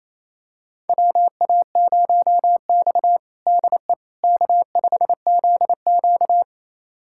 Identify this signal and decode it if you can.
Morse code, heard as a keyed tone, about 28 wpm, 700 Hz. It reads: WA0X DE K5ZQ
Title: USA Callsign DE Callsign - 4 Characters - 28wpm